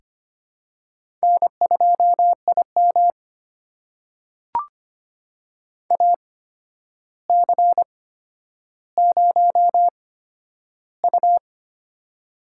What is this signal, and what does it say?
Morse code, heard as a keyed tone, about 25 wpm, spaced as for 11 wpm, 700 Hz. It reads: N2IM AC0U
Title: USA Call Signs - 8x Character Spacing - 25wpm